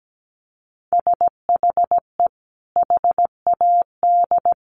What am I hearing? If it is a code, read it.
Morse code, heard as a keyed tone, about 17 wpm, 700 Hz. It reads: SHE HAD
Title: Sentences from Top 1500 Words 17wpm